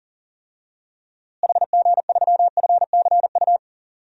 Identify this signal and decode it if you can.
Morse code, heard as a keyed tone, about 40 wpm, 700 Hz. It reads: HG3FCU
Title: DXCC-100 Call Signs - 6 Characters - Repeated 3x - 40wpm